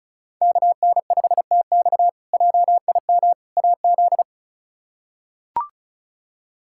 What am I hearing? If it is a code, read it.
Morse code, heard as a keyed tone, about 34 wpm, 700 Hz. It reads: KN5TX JIM AZ
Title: K1USN SST QSO Practice - 35wpm